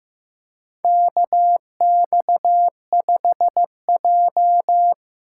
Morse code, heard as a keyed tone, about 16 wpm, 700 Hz. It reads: KX5J
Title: USA Call Signs - 4x Character Spacing - 15wpm